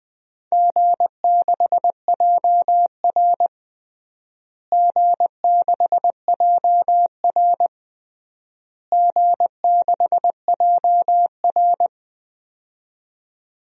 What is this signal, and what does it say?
Morse code, heard as a keyed tone, about 20 wpm, 700 Hz. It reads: G6JR G6JR G6JR
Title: DXCC-100 Call Signs - 4 Characters - Repeated 3x - 20wpm